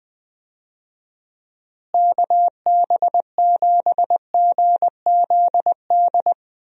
Morse code, heard as a keyed tone, about 20 wpm, 700 Hz. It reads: KB7GZD
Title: USA Call Signs - 6 Characters - Repeated 2x - 20wpm